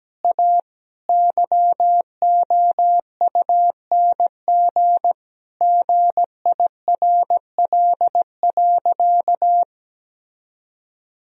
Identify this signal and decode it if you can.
Morse code, heard as a keyed tone, about 17 wpm, 700 Hz. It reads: A YOUNG GIRL.
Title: Sentences from Top 1000 Words - Review Part 4 - 17wpm